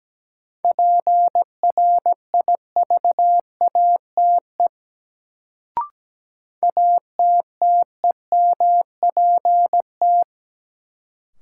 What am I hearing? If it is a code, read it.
Morse code, heard as a keyed tone, about 17 wpm, 700 Hz. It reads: PRIVATE ATTEMPT
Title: Sentences from Top 1000 Words - New Word Drill - 17wpm